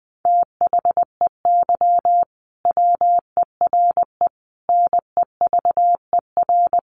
Morse code, heard as a keyed tone, about 20 wpm, 700 Hz. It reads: THEY WERE NEVER
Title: Sentences from Top 1400 Words 20wpm